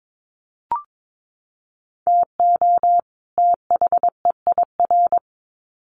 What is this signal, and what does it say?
Morse code, heard as a keyed tone, about 22 wpm, 700 Hz. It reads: TO THEIR
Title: Sets of 2 Words 22wpm